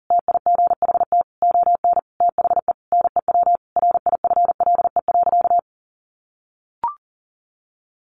Morse code, heard as a keyed tone, about 40 wpm, 700 Hz. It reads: TIGHT ON THE NEW RIFLE.
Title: A Sound Of Thunder 40wpm